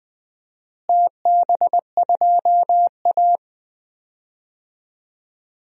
Morse code, heard as a keyed tone, about 20 wpm, 700 Hz. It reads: TB2A
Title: DXCC-100 Call Signs - 4 Characters - Repeated 3x - 20wpm